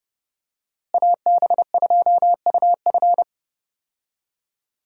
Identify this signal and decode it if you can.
Morse code, heard as a keyed tone, about 32 wpm, 700 Hz. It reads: AB2UF
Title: USA Call Signs - 5 Characters - Repeated 2x - 30wpm